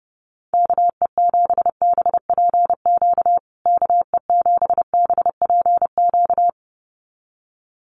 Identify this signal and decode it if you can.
Morse code, heard as a keyed tone, about 28 wpm, 700 Hz. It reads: KE7BPQ KE7BPQ
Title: USA Call Signs Repeated 2x 30wpm